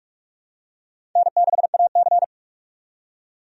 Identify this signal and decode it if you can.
Morse code, heard as a keyed tone, about 46 wpm, 700 Hz. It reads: N6AC
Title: USA Call Signs - 4 Characters - 45wpm